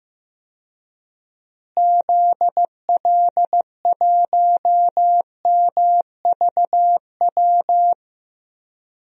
Morse code, heard as a keyed tone, about 16 wpm, 700 Hz. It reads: ZL1MVW
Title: DXCC-100 Call Signs - 6 Characters - Repeated 3x - 15wpm